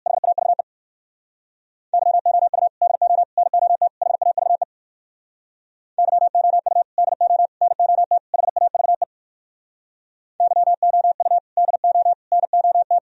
Morse code, heard as a keyed tone, about 58 wpm, 700 Hz. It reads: HAVE YOU DO NOT HAVE YOU DO NOT HAVE YOU DO NOT